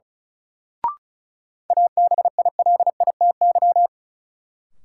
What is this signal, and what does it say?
Morse code, heard as a keyed tone, about 36 wpm, 700 Hz. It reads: ABILITY
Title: Sets of 3 Words - New Word Drill - 35wpm